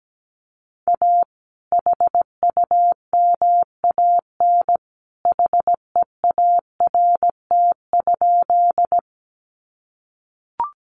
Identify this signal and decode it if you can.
Morse code, heard as a keyed tone, about 17 wpm, 700 Hz. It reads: A HUMAN HEART?
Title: Sentences from Top 1200 Words 17wpm